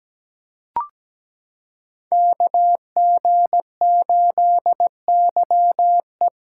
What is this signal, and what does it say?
Morse code, heard as a keyed tone, about 17 wpm, 700 Hz. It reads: KG8YE
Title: USA Callsign DE Callsign - 5 Characters - 17wpm